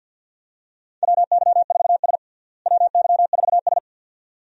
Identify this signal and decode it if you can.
Morse code, heard as a keyed tone, about 52 wpm, 700 Hz. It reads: WY4S WY4S
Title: USA Call Signs - 4 Characters - Repeated 2x - 50wpm